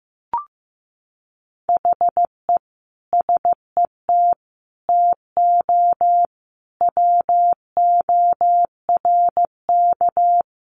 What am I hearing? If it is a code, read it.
Morse code, heard as a keyed tone, about 15 wpm, 700 Hz. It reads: HE SET TO WORK
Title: Sentences from Top 1200 Words 15wpm